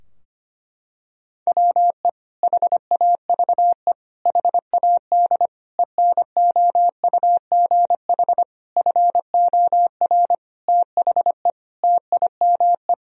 Morse code, heard as a keyed tone, about 25 wpm, 700 Hz. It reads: WE HAVE HAD ENOUGH FOR THE TIME
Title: Sentences from Top 1000 Words - Review Part 3 - 25wpm